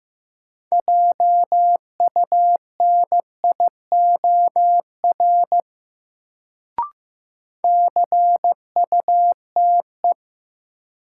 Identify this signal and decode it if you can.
Morse code, heard as a keyed tone, about 15 wpm, 700 Hz. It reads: JUNIOR CUTE
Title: Sentences from Top 1000 Words - Review Part 7 - New Word Drill - 15wpm